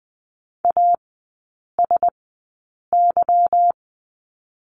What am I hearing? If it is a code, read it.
Morse code, heard as a keyed tone, about 20 wpm, spaced as for 8 wpm, 690 Hz. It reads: ASY
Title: 145 Lesson 24e - Letter y - Three Characters 20wpm